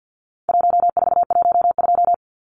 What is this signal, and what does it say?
Morse code, heard as a keyed tone, about 50 wpm, 700 Hz. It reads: 1412
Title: Numbers - 4-Digit 50wpm